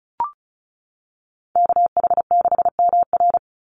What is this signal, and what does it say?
Morse code, heard as a keyed tone, about 34 wpm, 700 Hz. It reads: KH6MR
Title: USA Call Signs 35wpm